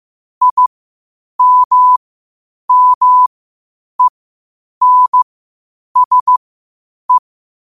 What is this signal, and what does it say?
Morse code, heard as a keyed tone, about 15 wpm, spaced as for 8 wpm, 1000 Hz. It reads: IMMENSE
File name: qotdfiles / 8wpm